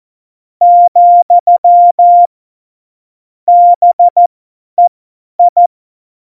Morse code, heard as a keyed tone, about 14 wpm, spaced as for 10 wpm, 700 Hz. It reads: , BEI
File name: Morsekurs / Witze / Witz 29. Dezember 2020, 50 F70